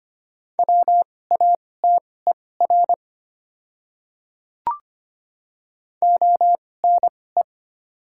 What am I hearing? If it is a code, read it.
Morse code, heard as a keyed tone, about 25 wpm, spaced as for 18 wpm, 690 Hz. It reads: WATER ONE
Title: 2x Character Spacing - Top 100 Words - 25wpm